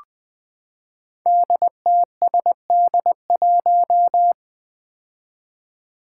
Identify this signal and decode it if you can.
Morse code, heard as a keyed tone, about 20 wpm, 700 Hz. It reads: DTSD1
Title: Five Challenging Letter-Number Combinations 20wpm